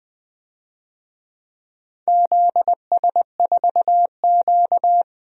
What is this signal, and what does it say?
Morse code, heard as a keyed tone, about 20 wpm, 700 Hz. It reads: ZS4Q
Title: DXCC-100 Call Signs - 4 Characters - 20wpm